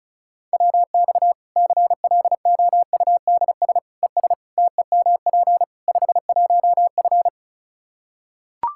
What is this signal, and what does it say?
Morse code, heard as a keyed tone, about 34 wpm, 700 Hz. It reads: WX CLOUDS ES TEMP 51F
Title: Practice QSO - Exchange 2 - Info - 35wpm